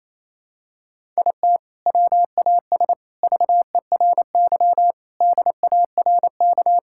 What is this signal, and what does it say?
Morse code, heard as a keyed tone, about 28 wpm, 700 Hz. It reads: IT WAS VERY DARK
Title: Sentences from Top 1000 Words - Review Part 6 - 28wpm